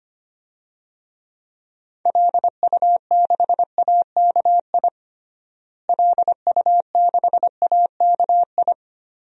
Morse code, heard as a keyed tone, about 25 wpm, 700 Hz. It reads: LU6AKI LU6AKI
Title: DXCC-100 Call Signs - 6 Characters - Repeated 3x - 25wpm